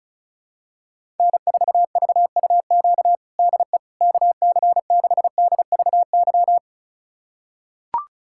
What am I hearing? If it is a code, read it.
Morse code, heard as a keyed tone, about 34 wpm, 700 Hz. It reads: N4VUQ DE KC6DVY
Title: USA Callsign DE Callsign 35wpm